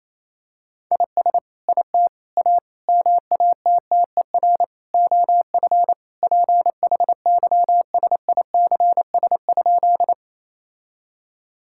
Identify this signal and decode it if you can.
Morse code, heard as a keyed tone, about 28 wpm, 700 Hz. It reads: IS IT A MATTER OF PHYSICS?
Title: Sentences from Top 1000 Words - Review Part 5 - 28wpm